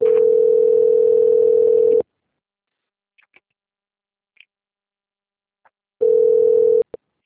Ringback tone (North America), 440 + 480 Hz.
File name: out- - - - - .37839